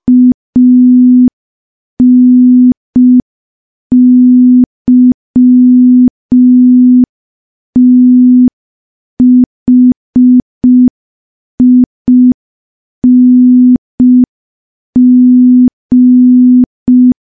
Morse code, heard as a keyed tone, about 5 wpm, 250 Hz. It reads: ANYTHING
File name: morse / morse-training-05WPM